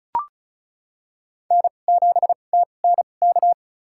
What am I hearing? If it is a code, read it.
Morse code, heard as a keyed tone, about 34 wpm, spaced as for 25 wpm, 700 Hz. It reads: N7TNK
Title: USA Call Signs - 2x Character Spacing - 35wpm